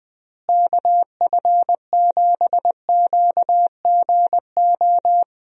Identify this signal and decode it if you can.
Morse code, heard as a keyed tone, about 20 wpm, 700 Hz. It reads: KF7QGO
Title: USA Call Signs - 6 Characters - Repeated 2x - 20wpm